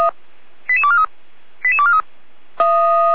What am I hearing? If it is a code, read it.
Five-tone selective call (ZVEI), heard as quick runs of five tones, tones of about 70 ms. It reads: 88243 88243